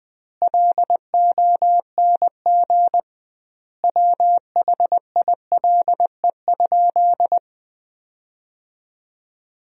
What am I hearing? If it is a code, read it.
Morse code, heard as a keyed tone, about 20 wpm, 700 Hz. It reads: LONG WHILE?
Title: 2x Word-Spacing - Sentences from Top 300 Words - 20wpm